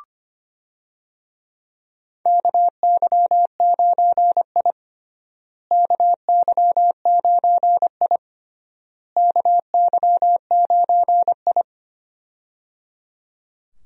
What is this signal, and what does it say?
Morse code, heard as a keyed tone, about 25 wpm, 700 Hz. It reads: KY9I KY9I KY9I
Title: USA Call Signs - 4 Characters - Repeated 3x - 25wpm